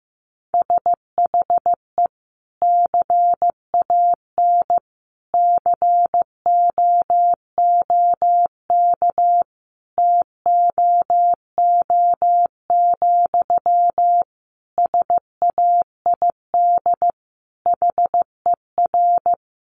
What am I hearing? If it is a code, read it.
Morse code, heard as a keyed tone, about 15 wpm, 700 Hz. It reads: SHE CAN COOK TOO, SAID HER
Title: Sentences from Top 700 Words 15wpm